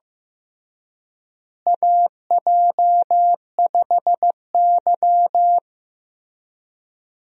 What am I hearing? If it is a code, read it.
Morse code, heard as a keyed tone, about 15 wpm, 700 Hz. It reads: AJ5Y
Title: USA Call Signs - 4 Characters - Repeated 3x - 15wpm